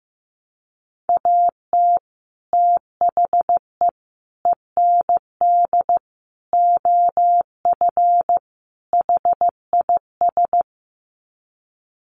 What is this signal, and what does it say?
Morse code, heard as a keyed tone, about 15 wpm, 700 Hz. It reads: AT THE END OF HIS
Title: Sets of 5 Words Encore 15wpm